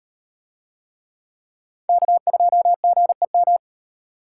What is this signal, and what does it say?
Morse code, heard as a keyed tone, about 36 wpm, 700 Hz. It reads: K2GEM